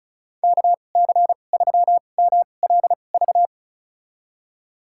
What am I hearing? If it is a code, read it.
Morse code, heard as a keyed tone, about 34 wpm, spaced as for 25 wpm, 700 Hz. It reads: KC3MLV
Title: USA Call Signs - 6 Characters - 2x Character Spacing - 35wpm